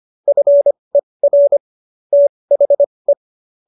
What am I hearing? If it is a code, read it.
Morse code, heard as a keyed tone, about 25 wpm, spaced as for 20 wpm, 570 Hz. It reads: FER THE